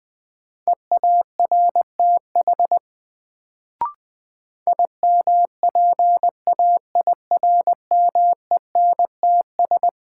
Morse code, heard as a keyed tone, about 20 wpm, 700 Hz. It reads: EARTH IMPAIRMENTS
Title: Sets of 5 Words Encore - New Word Drill - 20wpm